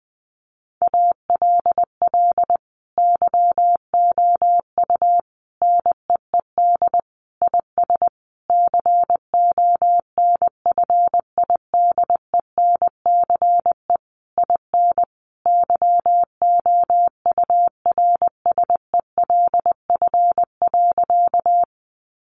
Morse code, heard as a keyed tone, about 20 wpm, 700 Hz. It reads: ALL YOU NEED IS CONFIDENCE IN YOURSELF.
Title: Sentences from Top 1500 Words 20wpm